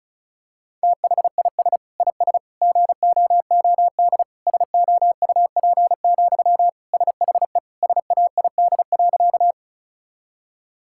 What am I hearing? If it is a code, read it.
Morse code, heard as a keyed tone, about 34 wpm, 690 Hz. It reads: THIS IS GOOD SOUP, SHE SAID.
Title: Sentences from Top 1000 Words - Review Part 6 - 35wpm